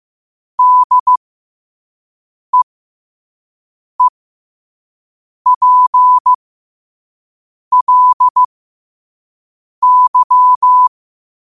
Morse code, heard as a keyed tone, about 15 wpm, spaced as for 5 wpm, 1000 Hz. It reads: DEEPLY
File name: qotdfiles / 5wpm